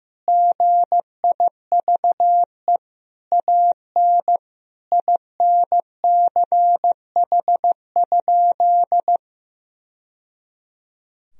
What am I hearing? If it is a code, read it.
Morse code, heard as a keyed tone, about 15 wpm, 700 Hz. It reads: GIVE AN INCH?
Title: Sentences from Top 1000 Words - Review Part 3 - 15wpm